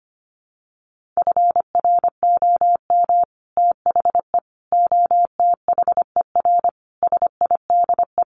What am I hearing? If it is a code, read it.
Morse code, heard as a keyed tone, about 25 wpm, 700 Hz. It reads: FROM THE OTHER SIDE
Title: Sets of 5 Words Encore 25wpm